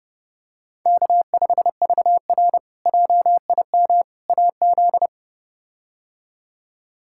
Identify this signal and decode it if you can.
Morse code, heard as a keyed tone, about 30 wpm, 700 Hz. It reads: K5VR JIM AZ
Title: K1USN SST QSO Practice - 30wpm